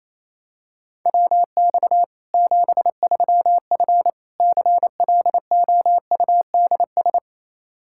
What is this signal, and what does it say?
Morse code, heard as a keyed tone, about 27 wpm, 700 Hz. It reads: WX 73F CLOUDS
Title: Practice QSO - Exchange 2 - Info - 28wpm